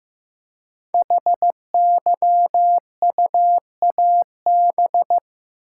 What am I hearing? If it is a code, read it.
Morse code, heard as a keyed tone, about 15 wpm, 700 Hz. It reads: HYUAB
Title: Code Groups - 5 Letter:Numbers - 15wpm